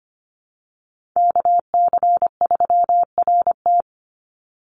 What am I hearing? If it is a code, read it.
Morse code, heard as a keyed tone, about 25 wpm, 700 Hz. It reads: KC3RT
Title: NA SOTA Chaser Callsigns 25wpm